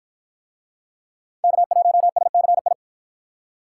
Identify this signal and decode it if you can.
Morse code, heard as a keyed tone, about 53 wpm, 700 Hz. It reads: K1SCI